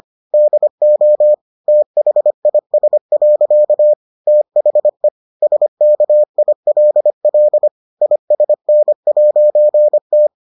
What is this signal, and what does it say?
Morse code, heard as a keyed tone, about 25 wpm, 600 Hz. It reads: DO THIS. THE SKILL ISN'T